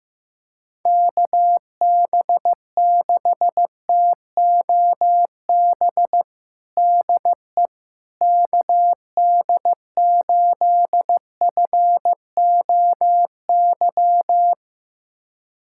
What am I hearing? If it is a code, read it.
Morse code, heard as a keyed tone, about 15 wpm, 700 Hz. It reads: KB6TOB DE KD8FOY
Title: USA Callsign DE Callsign 15wpm